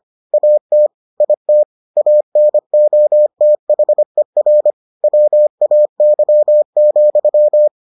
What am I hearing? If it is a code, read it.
Morse code, heard as a keyed tone, about 25 wpm, 600 Hz. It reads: AT IT ANOTHER WAY,